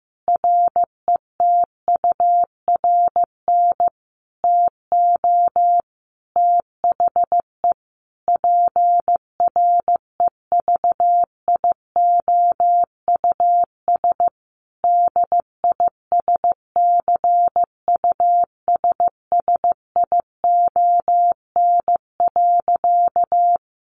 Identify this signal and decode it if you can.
Morse code, heard as a keyed tone, about 15 wpm, 700 Hz. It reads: RETURN TO THE PREVIOUS DISCUSSION.
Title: Sentences from Top 1500 Words 15wpm